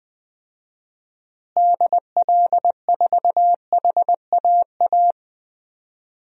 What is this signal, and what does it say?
Morse code, heard as a keyed tone, about 20 wpm, 700 Hz. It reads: DL4HAA
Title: DXCC-100 Call Signs - 6 Characters - Repeated 3x - 20wpm